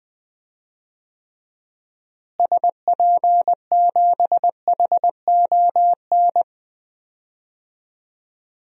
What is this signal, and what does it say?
Morse code, heard as a keyed tone, about 20 wpm, 700 Hz. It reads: SP7HON
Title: DXCC-100 Call Signs - 6 Characters - Repeated 2x - 20wpm